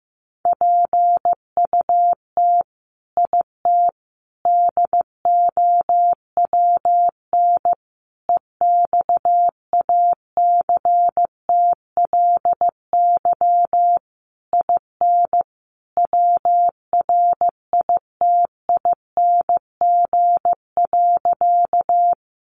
Morse code, heard as a keyed tone, about 15 wpm, 700 Hz. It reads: PUT IT DOWN EXACTLY IN WRITING.
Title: Sentences from Top 900 Words 15wpm